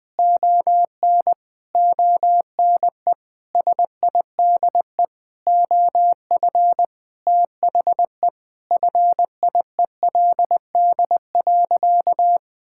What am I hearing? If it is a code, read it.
Morse code, heard as a keyed tone, about 20 wpm, 700 Hz. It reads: ON ONE SIDE OF THE FIELD.
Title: Sentences with 5-Word Phrases - 20wpm